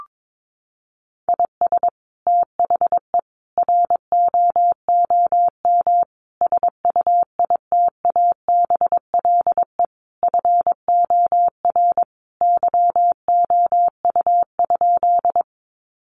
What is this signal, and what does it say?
Morse code, heard as a keyed tone, about 22 wpm, 700 Hz. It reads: IS THE ROOM SUITABLE FOR YOU?
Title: Sentences from Top 1400 Words 22wpm